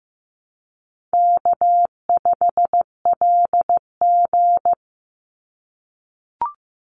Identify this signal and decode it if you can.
Morse code, heard as a keyed tone, about 15 wpm, 700 Hz. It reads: K5LG
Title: Easy Call Signs - 15wpm